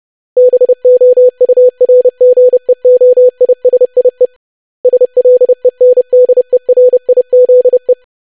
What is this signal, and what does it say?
Morse code, heard as a keyed tone, about 30 wpm, 500 Hz. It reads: BOURGEOISIE SLENDERIZE
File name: lesson / web / longwords-25wpm